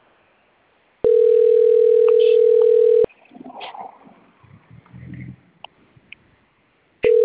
Ringback tone (North America), 440 + 480 Hz.